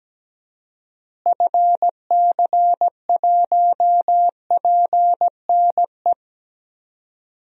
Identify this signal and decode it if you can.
Morse code, heard as a keyed tone, about 17 wpm, 700 Hz. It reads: FC1PNE
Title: DXCC-100 Call Signs - 6 Characters - Repeated 2x - 17wpm